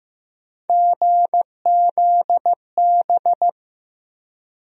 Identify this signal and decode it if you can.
Morse code, heard as a keyed tone, about 15 wpm, 700 Hz. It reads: GZB
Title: USA Call Sign Postfixes - 3 Letters 15wpm